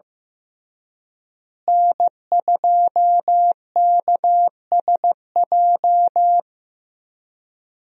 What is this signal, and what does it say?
Morse code, heard as a keyed tone, about 15 wpm, 700 Hz. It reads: N2KSJ
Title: USA Call Signs - 5 Characters - Repeated 3x - 15wpm